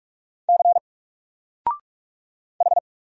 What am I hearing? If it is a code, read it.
Morse code, heard as a keyed tone, about 44 wpm, 700 Hz. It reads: C H
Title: Single Letter-Number Alt 45wpm